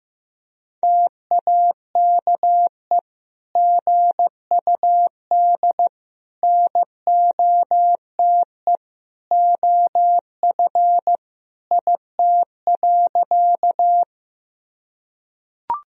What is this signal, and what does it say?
Morse code, heard as a keyed tone, about 15 wpm, 700 Hz. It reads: TAKE GUD NOTE OF IT.
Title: Sentences from Top 500 Words and Abbreviations 15wpm